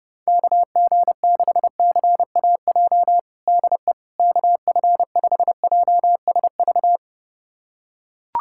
Morse code, heard as a keyed tone, about 30 wpm, 700 Hz. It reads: KG6CAJ DE KF5JSV
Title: USA Callsign DE Callsign - 6 Characters - 30wpm